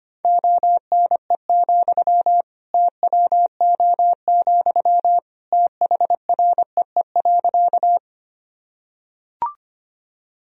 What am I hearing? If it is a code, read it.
Morse code, heard as a keyed tone, about 25 wpm, 700 Hz. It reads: ONE, TWO, THREE.
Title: Sentences from Top 1000 Words - Review Part 1 - 25wpm